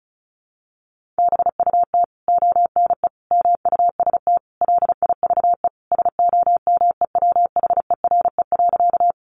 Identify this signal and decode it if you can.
Morse code, heard as a keyed tone, about 34 wpm, 700 Hz. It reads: BUT ONE MUST LIVE SOMEWHERE.
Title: Sentences from Top 1400 Words 35wpm